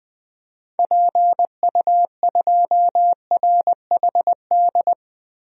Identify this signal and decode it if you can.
Morse code, heard as a keyed tone, about 20 wpm, 700 Hz. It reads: PU2RHD
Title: DXCC-100 Call Signs - 6 Characters - 20wpm